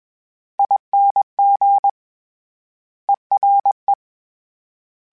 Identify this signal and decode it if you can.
Morse code, heard as a keyed tone, about 21 wpm, 800 Hz. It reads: ING ERE